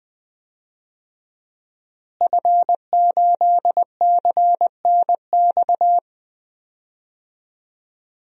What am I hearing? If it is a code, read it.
Morse code, heard as a keyed tone, about 20 wpm, 700 Hz. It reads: F8CNX
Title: DXCC-100 Call Signs - 5 Characters - Repeated 2x - 20wpm